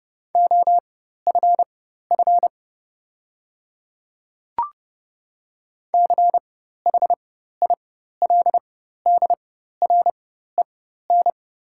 Morse code, heard as a keyed tone, about 32 wpm, spaced as for 14 wpm, 700 Hz. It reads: OFF CHILDREN
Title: 4x Character Spacing - Top 300 Words - 30wpm